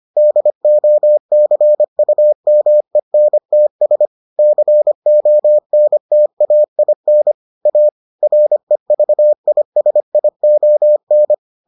Morse code, heard as a keyed tone, about 25 wpm, 600 Hz. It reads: DOCUMENTS CONTAIN A REVISION